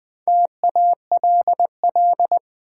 Morse code, heard as a keyed tone, about 20 wpm, 700 Hz. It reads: TALL
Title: Four Letter Words - 20wpm